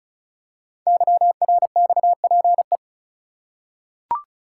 Morse code, heard as a keyed tone, about 33 wpm, 690 Hz. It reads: YRXPE
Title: Code Groups - 5 Letters - 35wpm